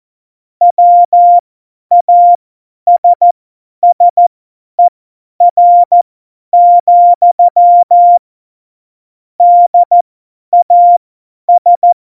Morse code, heard as a keyed tone, about 14 wpm, spaced as for 10 wpm, 700 Hz. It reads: WASSER, DAS
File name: Morsekurs / Witze / Witz 12. November 2019, 50 F70